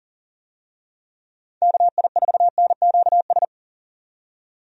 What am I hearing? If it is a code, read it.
Morse code, heard as a keyed tone, about 37 wpm, 700 Hz. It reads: KI4NQS
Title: USA Call Signs - 6 Characters - 40wpm